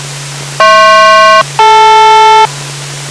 Aircraft SELCAL, heard as two pairs of tones. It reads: HP-DL